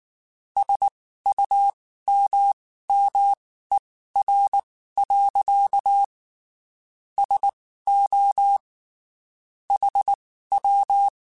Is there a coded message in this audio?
Yes: Morse code from 0.6 to 11.1 s, heard as a keyed tone, about 19 wpm, spaced as for 14 wpm, 800 Hz. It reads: SUMMER. SO HW